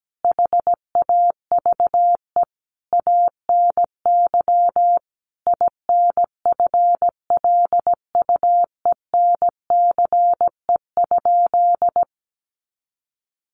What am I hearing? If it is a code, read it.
Morse code, heard as a keyed tone, about 17 wpm, 700 Hz. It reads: HAVE ANY INFLUENCE?
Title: Sentences from Top 1100 Words 17wpm